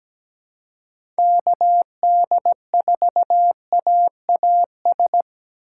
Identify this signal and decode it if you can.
Morse code, heard as a keyed tone, about 17 wpm, 700 Hz. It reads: KD4AAS
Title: USA Call Signs - 6 Characters - 3x Character Spacing - 17wpm